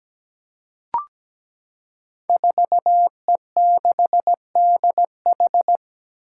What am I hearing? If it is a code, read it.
Morse code, heard as a keyed tone, about 17 wpm, 700 Hz. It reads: 4E6DH
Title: Five Challenging Letter-Number Combinations 17wpm